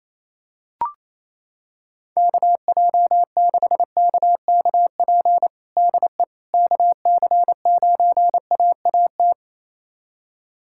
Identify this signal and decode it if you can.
Morse code, heard as a keyed tone, about 28 wpm, 700 Hz. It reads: KJ6KKP DE KC9AAT
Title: USA Callsign DE Callsign - 6 Characters - 28wpm